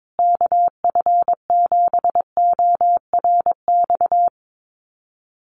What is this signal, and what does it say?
Morse code, heard as a keyed tone, about 22 wpm, 700 Hz. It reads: KF7ORX
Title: USA Call Signs Repeated 3x 22wpm